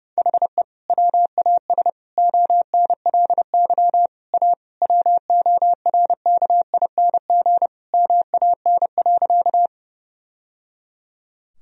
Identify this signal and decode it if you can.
Morse code, heard as a keyed tone, about 30 wpm, 700 Hz. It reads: HE WAS ONLY A WORKING MAN.
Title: Sentences from Top 1000 Words - Review Part 4 - 30wpm